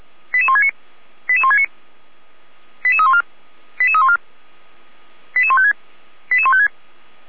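Five-tone selective call (ZVEI), heard as quick runs of five tones, tones of about 70 ms. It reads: 88169 88169 88314 88314 88156 88156
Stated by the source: ZVEI five-tone call received with an SDR.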